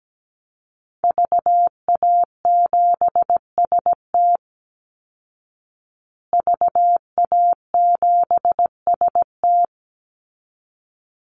Morse code, heard as a keyed tone, about 17 wpm, 700 Hz. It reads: VA7ST VA7ST
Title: Call Signs Repeated 3x - 17wpm